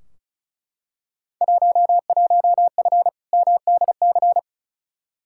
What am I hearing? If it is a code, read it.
Morse code, heard as a keyed tone, about 36 wpm, 700 Hz. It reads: 11F MDC
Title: Field Day Exchanges Encore - 35wpm